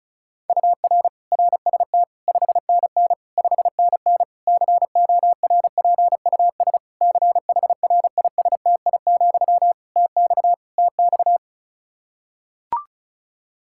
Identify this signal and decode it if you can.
Morse code, heard as a keyed tone, about 36 wpm, 700 Hz. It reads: UR RST 5NN 5NN CORPUS CHRISTI, TX TX
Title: K1USN SST QSO Practice 35wpm